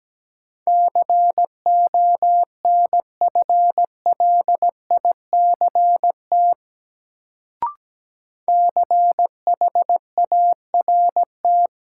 Morse code, heard as a keyed tone, about 17 wpm, 700 Hz. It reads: CONFLICT CHART
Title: Sentences from Top 1300 Words - New Word Drill Alt - 17wpm